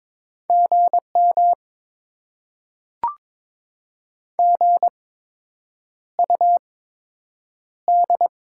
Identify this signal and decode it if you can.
Morse code, heard as a keyed tone, about 22 wpm, spaced as for 8 wpm, 700 Hz. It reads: GM GUD
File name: Top 20 QSO Elements - 8x Character Spacing - 22wpm